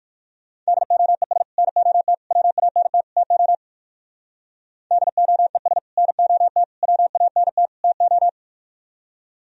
Morse code, heard as a keyed tone, about 55 wpm, 700 Hz. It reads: DOES NOT WANT TO DOES NOT WANT TO
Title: Speed-Racing - Sets of 4 Words 45wpm